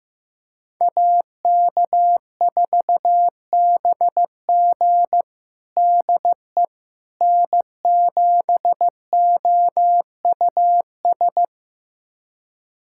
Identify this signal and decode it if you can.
Morse code, heard as a keyed tone, about 15 wpm, 700 Hz. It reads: AK4BG DE N7OUS
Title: USA Callsign DE Callsign - 5 Characters - 15wpm